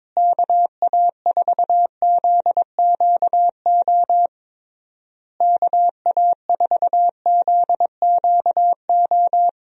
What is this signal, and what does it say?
Morse code, heard as a keyed tone, about 22 wpm, 700 Hz. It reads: KA4ZQO KA4ZQO
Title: USA Call Signs - 6 Characters - Repeated 3x - 22wpm